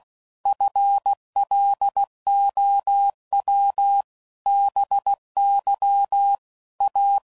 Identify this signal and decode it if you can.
Morse code, heard as a keyed tone, about 16 wpm, 800 Hz. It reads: FLOW BY A
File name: CW Code Groups @15WPM / Technical Subjects / XL